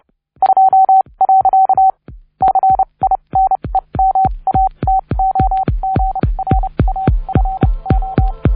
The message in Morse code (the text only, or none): 2. CINEMATOGRAFO